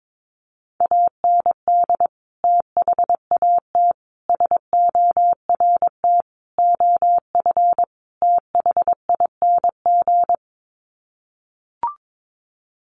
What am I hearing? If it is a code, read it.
Morse code, heard as a keyed tone, about 22 wpm, 700 Hz. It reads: AND THAT SORT OF THING
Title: Sets of 5 Words Encore2 22wpm